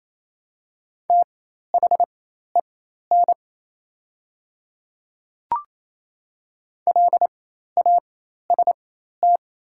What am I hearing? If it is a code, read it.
Morse code, heard as a keyed tone, about 28 wpm, spaced as for 13 wpm, 700 Hz. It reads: THEN LAST
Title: 4x Character Spacing - Top 300 Words - 28wpm